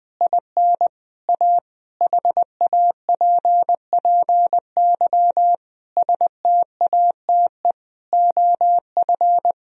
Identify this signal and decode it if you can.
Morse code, heard as a keyed tone, about 20 wpm, 700 Hz. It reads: IN A HAPPY STATE OF
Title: Sentences from Top 1000 Words - Review Part 2 - 20wpm